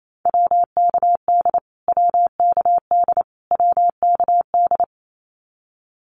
Morse code, heard as a keyed tone, about 28 wpm, 700 Hz. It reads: WKD WKD WKD
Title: Abbreviations Repeated 3x - 28wpm